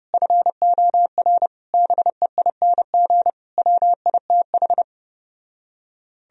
Morse code, heard as a keyed tone, about 33 wpm, 700 Hz. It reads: FOR BEING WITH